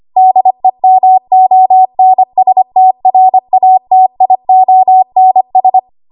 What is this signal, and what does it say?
Morse code, heard as a keyed tone, about 25 wpm, 750 Hz. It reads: DEMONSTRATIONS
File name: Morse / 25WPM